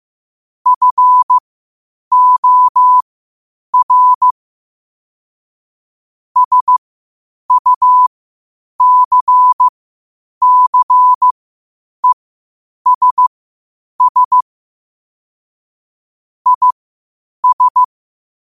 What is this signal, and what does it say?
Morse code, heard as a keyed tone, about 15 wpm, spaced as for 8 wpm, 1000 Hz. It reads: FOR SUCCESS IS
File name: qotdfiles / 8wpm